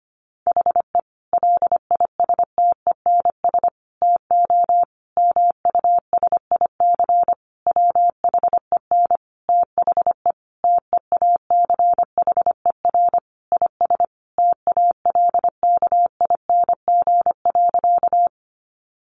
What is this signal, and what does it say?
Morse code, heard as a keyed tone, about 25 wpm, 700 Hz. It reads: HE LISTENS TO MUSIC WHEN THE TEACHER IS TALKING.
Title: Additional Easy Phrases 25wpm